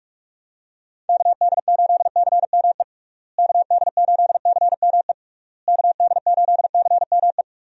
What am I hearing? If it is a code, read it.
Morse code, heard as a keyed tone, about 46 wpm, 700 Hz. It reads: KD8CME KD8CME KD8CME
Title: USA Call Signs - 6 Characters - Repeated 3x - 45wpm